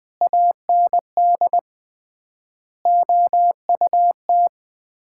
Morse code, heard as a keyed tone, about 21 wpm, 700 Hz. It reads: AND OUT
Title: 3x Word-Spacing - Sets of 4 Words - 20wpm